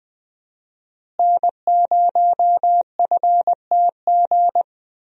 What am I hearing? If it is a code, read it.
Morse code, heard as a keyed tone, about 20 wpm, 700 Hz. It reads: N0FTG
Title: USA Call Signs - 5 Characters - Repeated 2x - 20wpm